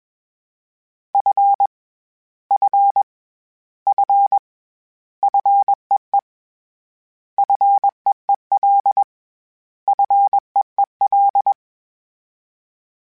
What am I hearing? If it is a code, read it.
Morse code, heard as a keyed tone, about 21 wpm, 800 Hz. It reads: F F F FEE FEEL FEEL
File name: Short Runs / Short Runs-mp3s / SinglePrefixes @20